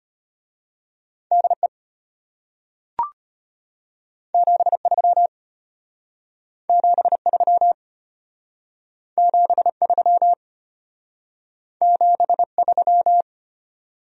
Morse code, heard as a keyed tone, about 33 wpm, 700 Hz. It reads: DE 73 73 73 73